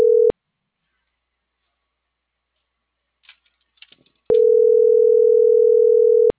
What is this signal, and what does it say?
Ringback tone (North America), 440 + 480 Hz.